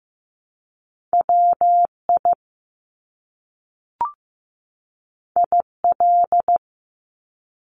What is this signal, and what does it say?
Morse code, heard as a keyed tone, about 15 wpm, 700 Hz. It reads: WI IL
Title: US States 15wpm